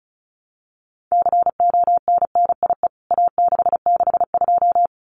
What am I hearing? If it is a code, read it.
Morse code, heard as a keyed tone, about 34 wpm, 700 Hz. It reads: CONNIE A662
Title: CWT Exchange 35wpm